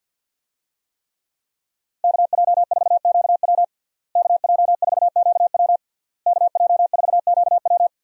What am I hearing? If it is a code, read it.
Morse code, heard as a keyed tone, about 48 wpm, 700 Hz. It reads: KJ4QW KJ4QW KJ4QW
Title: USA Call Signs - 5 Characters - Repeated 3x - 50wpm